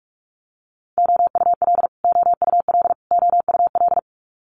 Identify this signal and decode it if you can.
Morse code, heard as a keyed tone, about 46 wpm, 690 Hz. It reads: OUL OUL OUL
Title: Multi-letter Phonemes Repeated 3x 45wpm